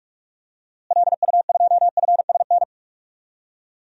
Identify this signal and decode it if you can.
Morse code, heard as a keyed tone, about 46 wpm, 700 Hz. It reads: LU2FSN
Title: DXCC-100 Call Signs - 6 Characters - Repeated 3x - 45wpm